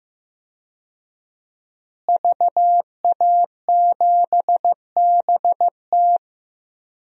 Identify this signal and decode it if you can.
Morse code, heard as a keyed tone, about 15 wpm, 700 Hz. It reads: VA7BT
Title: DXCC-100 Call Signs - 5 Characters - Repeated 2x - 15wpm